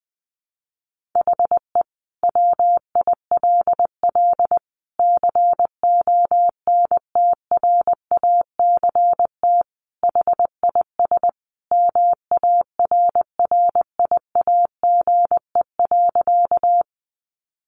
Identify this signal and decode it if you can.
Morse code, heard as a keyed tone, about 20 wpm, 700 Hz. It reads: HE WILL CONTRACT HIS MARRIAGE.
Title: Sentences from Top 1400 Words 20wpm